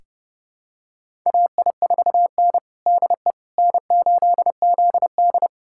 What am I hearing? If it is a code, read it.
Morse code, heard as a keyed tone, about 32 wpm, 700 Hz. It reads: AI4N DE N8ZD
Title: USA Callsign DE Callsign - 4 Characters - 30wpm